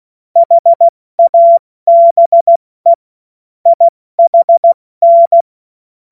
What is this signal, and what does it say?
Morse code, heard as a keyed tone, about 16 wpm, 680 Hz. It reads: HABE IHN
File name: Morsekurs / Witze / Witz 29. Juni 2021, 70 F80